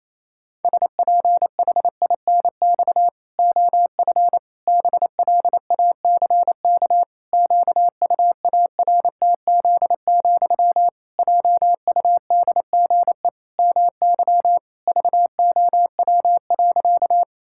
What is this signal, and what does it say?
Morse code, heard as a keyed tone, about 28 wpm, 700 Hz. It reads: SPHINX OF BLACK QUARTZ, JUDGE MY VOW.
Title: Pangram Challenge v2 - 28wpm